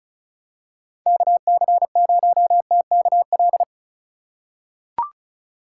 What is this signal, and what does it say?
Morse code, heard as a keyed tone, about 34 wpm, 700 Hz. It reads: KC0TKL
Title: USA Call Signs - 6 Characters - 4x Character Spacing - 35wpm